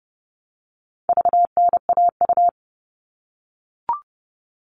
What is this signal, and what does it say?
Morse code, heard as a keyed tone, about 30 wpm, 700 Hz. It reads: VNAU
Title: All Cut Numbers - 4-Digit 30wpm